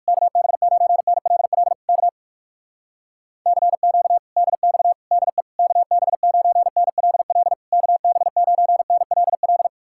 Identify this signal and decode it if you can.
Morse code, heard as a keyed tone, about 62 wpm, spaced as for 55 wpm, 700 Hz. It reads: KB9NLL K CQ DX DE KB9NLL KB9NLL
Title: Speed-Racing Common QSO Phrases - 45wpm